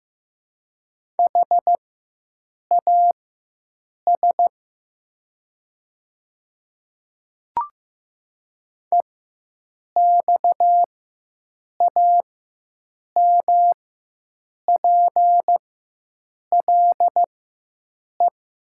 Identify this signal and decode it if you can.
Morse code, heard as a keyed tone, about 15 wpm, spaced as for 7 wpm, 700 Hz. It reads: HAS EXAMPLE
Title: 4x Character Spacing - Top 300 Words - 15wpm